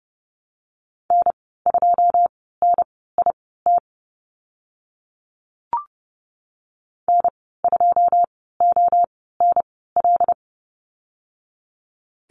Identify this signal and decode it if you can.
Morse code, heard as a keyed tone, about 30 wpm, spaced as for 17 wpm, 700 Hz. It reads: N2NIT N2ONL
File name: Lesson 14g - Number 2 - Call Signs 30wpm